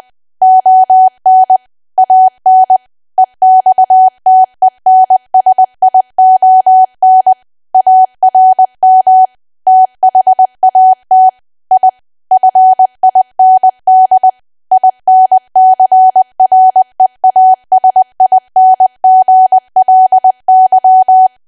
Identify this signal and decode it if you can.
Morse code, heard as a keyed tone, about 20 wpm, 750 Hz. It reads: ON AN EXTENSION ARM THAT I FIND INCREASINGLY
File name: cwa / W1AW-Level3 / 20WPM